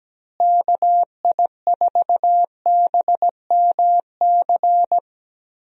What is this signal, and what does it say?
Morse code, heard as a keyed tone, about 17 wpm, 700 Hz. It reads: KI4BMC
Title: USA Call Signs - 6 Characters - Repeated 3x - 17wpm